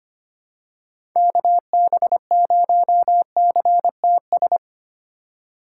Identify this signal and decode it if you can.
Morse code, heard as a keyed tone, about 25 wpm, 700 Hz. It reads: KB0CTS
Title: USA Call Signs - 4x Character Spacing - 25wpm